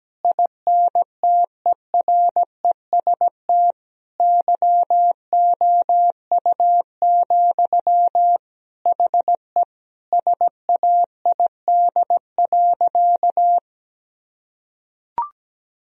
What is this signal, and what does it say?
Morse code, heard as a keyed tone, about 17 wpm, 700 Hz. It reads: INTEREST YOU, HE SAID.
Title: Sentences from Top 1000 Words - Review Part 2 - 17wpm